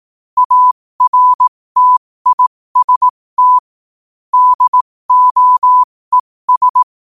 Morse code, heard as a keyed tone, about 18 wpm, spaced as for 15 wpm, 1000 Hz. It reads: ARTIST DOES
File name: qotdfiles / 15wpm